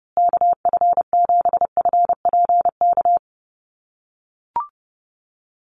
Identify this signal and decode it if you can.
Morse code, heard as a keyed tone, about 30 wpm, 700 Hz. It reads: KF7FPK
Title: USA Call Signs Repeated 2x 30wpm